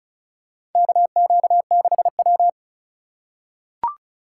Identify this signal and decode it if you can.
Morse code, heard as a keyed tone, about 36 wpm, 700 Hz. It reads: KQ6W
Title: USA Call Signs - 4 Characters - 4x Character Spacing - 35wpm